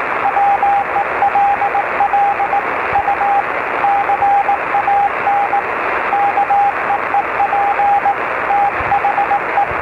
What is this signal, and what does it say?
Morse code, heard as a keyed tone, about 19 wpm, 800 Hz, with radio noise. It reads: WELL U CAN KEEP THE